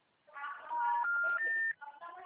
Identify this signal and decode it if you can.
Special information tone, with speech.